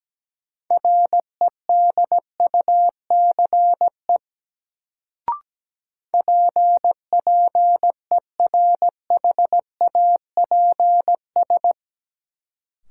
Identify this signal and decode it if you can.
Morse code, heard as a keyed tone, about 17 wpm, 700 Hz. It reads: REDUCE PPERHAPS
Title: Sentences from Top 1100 Words - New Word Drill - 17wpm